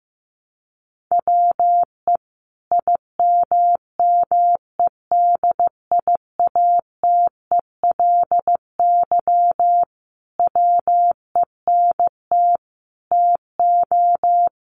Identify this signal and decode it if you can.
Morse code, heard as a keyed tone, about 15 wpm, 700 Hz. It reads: WE IMMEDIATELY WENT TO
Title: Sentences from Top 1100 Words 15wpm